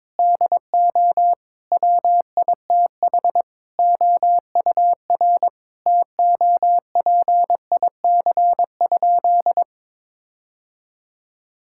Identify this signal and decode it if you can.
Morse code, heard as a keyed tone, about 22 wpm, 700 Hz. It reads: DO WITH OUR TOPIC?
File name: practice / 22wpm / Sentences with 8-Word Phrases - 22wpm